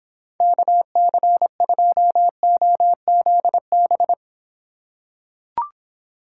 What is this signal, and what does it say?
Morse code, heard as a keyed tone, about 26 wpm, 700 Hz. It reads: KC2OZB